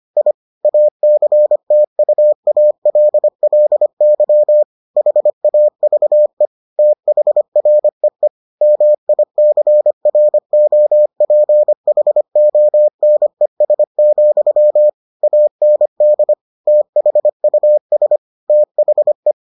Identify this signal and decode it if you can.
Morse code, heard as a keyed tone, about 25 wpm, 600 Hz. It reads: I ACTUALLY HAVE THREE MICROPHONES, AND THUS THE